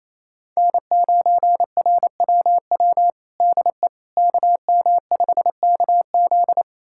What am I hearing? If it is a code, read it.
Morse code, heard as a keyed tone, about 28 wpm, 700 Hz. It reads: N9RWW DE KM5KZ
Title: USA Callsign DE Callsign - 5 Characters - 28wpm